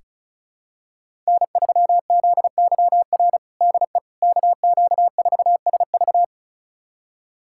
Morse code, heard as a keyed tone, about 36 wpm, 700 Hz. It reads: N3ZYR DE KQ4SV
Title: USA Callsign DE Callsign - 5 Characters - 35wpm